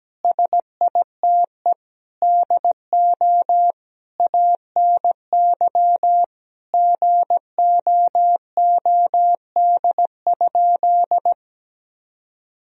Morse code, heard as a keyed tone, about 17 wpm, 700 Hz. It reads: SITE DO ANY GOOD?
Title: Sentences from Top 1000 Words - Review Part 4 - 17wpm